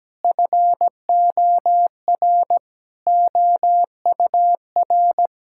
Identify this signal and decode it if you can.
Morse code, heard as a keyed tone, about 17 wpm, 700 Hz. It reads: FOR OUR
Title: Sentences from Top 1000 Words - Review Part 5 - 17wpm